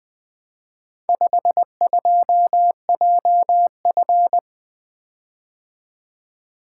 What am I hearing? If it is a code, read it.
Morse code, heard as a keyed tone, about 21 wpm, 700 Hz. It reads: 52JF
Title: Speed-Racing - Four Letter-Number Combinations - 20wpm